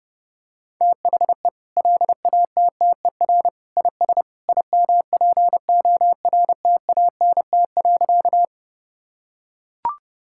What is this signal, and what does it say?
Morse code, heard as a keyed tone, about 30 wpm, 700 Hz. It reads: THE LATTER IS IMPORTANT.
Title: Sentences from Top 1600 Words 30wpm